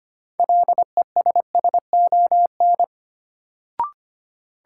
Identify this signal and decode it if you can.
Morse code, heard as a keyed tone, about 25 wpm, 700 Hz. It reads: LESSON
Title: Sentences from Top 1200 Words - New Word Drill - 25wpm